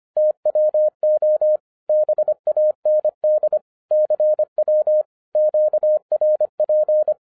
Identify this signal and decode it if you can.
Morse code, heard as a keyed tone, about 24 wpm, 610 Hz. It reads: TWO BAND CW QRP
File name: 2011-07-mp3 / n24 QRP OTAKU 25wpm